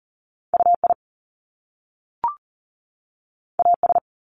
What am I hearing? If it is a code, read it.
Morse code, heard as a keyed tone, about 38 wpm, 700 Hz. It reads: UI AS
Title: Top Two Letter Combinations in Words 40wpm